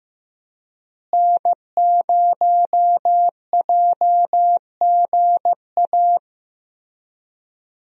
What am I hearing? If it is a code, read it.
Morse code, heard as a keyed tone, about 15 wpm, 700 Hz. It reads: N0JGA
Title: USA Call Signs - 5 Characters - Repeated 3x - 15wpm